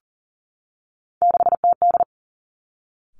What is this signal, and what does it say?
Morse code, heard as a keyed tone, about 41 wpm, 700 Hz. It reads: 6TD
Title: Three Challenging Letter-Number Combinations 40wpm